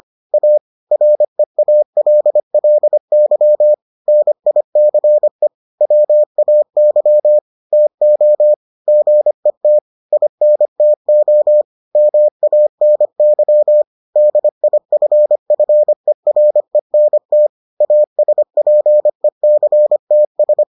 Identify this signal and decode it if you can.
Morse code, heard as a keyed tone, about 25 wpm, 600 Hz. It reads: A REALLY NICE WAY TO GET INTO MANY DIFFERENT ASPECTS